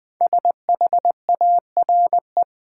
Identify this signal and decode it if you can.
Morse code, heard as a keyed tone, about 20 wpm, 700 Hz. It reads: SHARE
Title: Sentences from Top 900 Words - New Word Drill - 20wpm